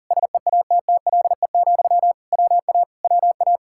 Morse code, heard as a keyed tone, about 49 wpm, 700 Hz. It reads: SEATTLE, WA WA
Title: Speed-Racing Common QSO Phrases - 40wpm